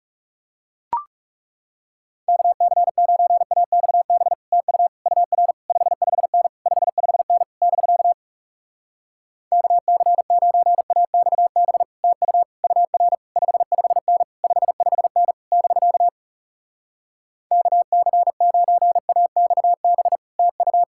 Morse code, heard as a keyed tone, about 40 wpm, 700 Hz. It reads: KC9AXB TU UR 55N 55N <BK> KC9AXB TU UR 55N 55N <BK> KC9AXB TU